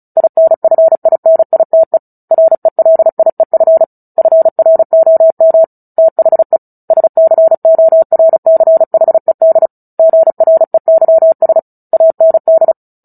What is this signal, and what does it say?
Morse code, heard as a keyed tone, about 34 wpm, 650 Hz. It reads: INFINITE RELIEF FROM THE SCORCHED GREYS AND
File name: Book Two / 09-Chapter 9 - 35 WPM